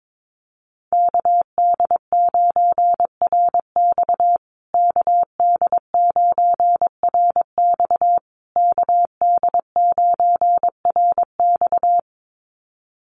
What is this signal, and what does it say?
Morse code, heard as a keyed tone, about 22 wpm, 700 Hz. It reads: KD9RX KD9RX KD9RX
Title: USA Call Signs Repeated 3x 22wpm